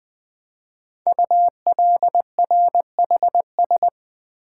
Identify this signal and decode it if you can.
Morse code, heard as a keyed tone, about 20 wpm, 700 Hz. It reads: ULRHS
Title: Code Groups - 5 Letters - 20wpm